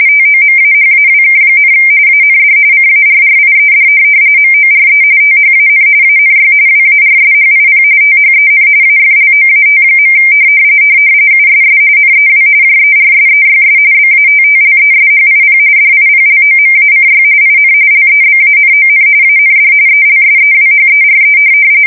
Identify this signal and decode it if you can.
RTTY, about 45.45 baud. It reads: TOMAHAWK, AND CARRIED MORE THAN ANY WARSHIP THEN AFLOAT. USS NEW JERSEYS BATTLESHIP DIPLOMACY IN LEBANON AND ELSEWHERE
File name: fsk170 / USS-New-Jersey